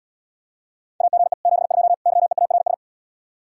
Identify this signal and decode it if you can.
Morse code, heard as a keyed tone, about 80 wpm, 700 Hz. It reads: AGE 72 YEARS